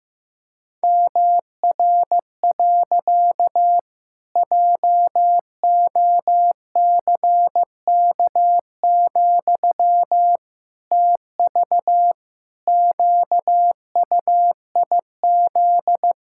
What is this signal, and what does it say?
Morse code, heard as a keyed tone, about 15 wpm, 700 Hz. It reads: MR. JOCK, TV QUIZ
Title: Pangram Challenge v2 - 15wpm